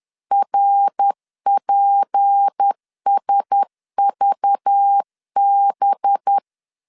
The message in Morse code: RPSVB

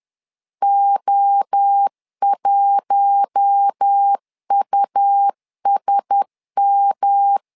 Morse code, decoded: O1USM

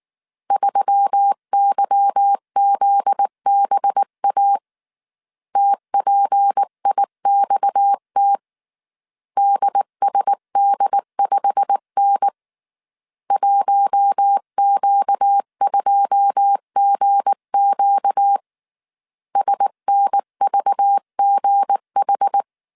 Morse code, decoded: 3YZBA TPIXT DSD5N 1Q2GQ SNVGH

